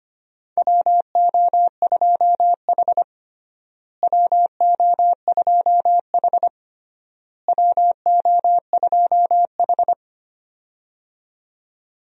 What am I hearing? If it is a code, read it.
Morse code, heard as a keyed tone, about 25 wpm, 700 Hz. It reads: WO2H WO2H WO2H
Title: USA Call Signs - 4 Characters - Repeated 3x - 25wpm